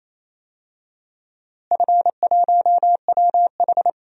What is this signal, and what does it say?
Morse code, heard as a keyed tone, about 28 wpm, 700 Hz. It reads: F1WH
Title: DXCC-100 Call Signs - 4 Characters - Repeated 3x - 28wpm